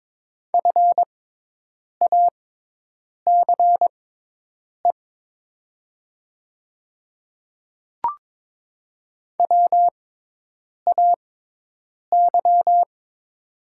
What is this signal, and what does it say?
Morse code, heard as a keyed tone, about 22 wpm, spaced as for 8 wpm, 700 Hz. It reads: FACE WAY
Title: 6x Character Spacing - Top 500 Words - 22wpm